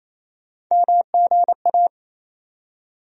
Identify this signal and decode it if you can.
Morse code, heard as a keyed tone, about 28 wpm, 700 Hz. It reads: MGA
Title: USA Call Sign Postfixes - 3 Letters 28wpm